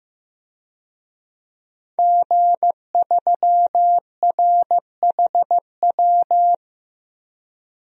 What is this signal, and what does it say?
Morse code, heard as a keyed tone, about 15 wpm, 700 Hz. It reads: G3RHW
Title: DXCC-100 Call Signs - 5 Characters - Repeated 3x - 15wpm